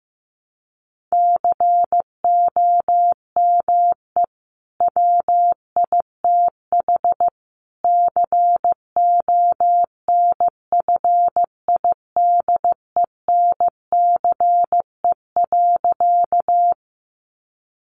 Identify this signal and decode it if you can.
Morse code, heard as a keyed tone, about 15 wpm, 700 Hz. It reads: COME WITH CONFIDENCE.
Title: Sentences from Top 1500 Words 15wpm